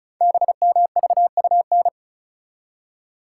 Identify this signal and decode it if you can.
Morse code, heard as a keyed tone, about 36 wpm, 700 Hz. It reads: BMVUN
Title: Code Groups - 5 Letters - 35wpm